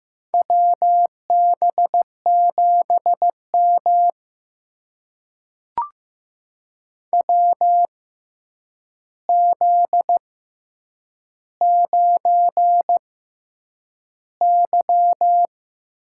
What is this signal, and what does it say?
Morse code, heard as a keyed tone, about 15 wpm, spaced as for 8 wpm, 700 Hz. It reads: WB7M WZ9Y
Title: USA Call Signs - 4 Characters - 6x Character Spacing - 15wpm